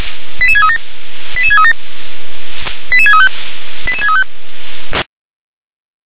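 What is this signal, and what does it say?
Five-tone selective call (ZVEI), heard as quick runs of five tones, tones of about 70 ms. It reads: 88637 88637 88635 88635
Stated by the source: ZVEI five-tone call received on an SDR.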